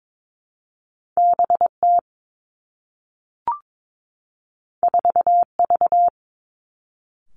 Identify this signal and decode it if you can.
Morse code, heard as a keyed tone, about 22 wpm, 700 Hz. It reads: B T 4 V
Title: Two Challenging Letter-Number Combinations 22wpm